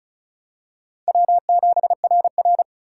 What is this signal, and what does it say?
Morse code, heard as a keyed tone, about 34 wpm, 700 Hz. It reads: W7RR
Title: USA Call Signs - 4 Characters - 3x Character Spacing - 35wpm